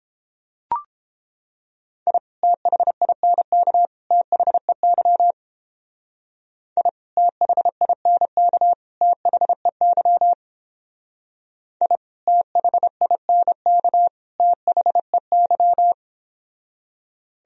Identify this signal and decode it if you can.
Morse code, heard as a keyed tone, about 31 wpm, 700 Hz. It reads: I THINK THEY I THINK THEY I THINK THEY